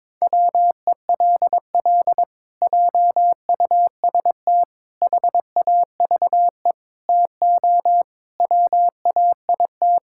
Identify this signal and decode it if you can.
Morse code, heard as a keyed tone, about 22 wpm, 700 Hz. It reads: WELL JUST HAVE TO WAIT
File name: practice / 22wpm / Sets of 7 Words - 22wpm